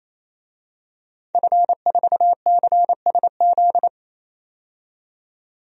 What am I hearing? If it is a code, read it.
Morse code, heard as a keyed tone, about 27 wpm, 700 Hz. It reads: F4CSZ
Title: DXCC-100 Call Signs - 5 Characters - 28wpm